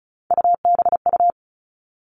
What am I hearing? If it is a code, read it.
Morse code, heard as a keyed tone, about 36 wpm, 700 Hz. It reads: UBU
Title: Three Challenging Letter-Number Combinations 35wpm